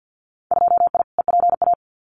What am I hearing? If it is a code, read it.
Morse code, heard as a keyed tone, about 50 wpm, 700 Hz. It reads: 2I EPA
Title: Winter Field Day Exchanges 50wpm